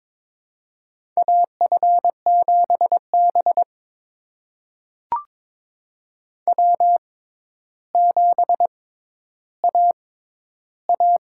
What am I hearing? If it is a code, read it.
Morse code, heard as a keyed tone, about 22 wpm, spaced as for 11 wpm, 700 Hz. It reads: AF7B W7AA
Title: USA Call Signs - 4 Characters - 6x Character Spacing - 22wpm